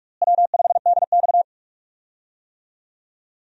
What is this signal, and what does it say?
Morse code, heard as a keyed tone, about 42 wpm, 700 Hz. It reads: W5DX
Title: USA Call Signs - 4 Characters - Repeated 2x - 45wpm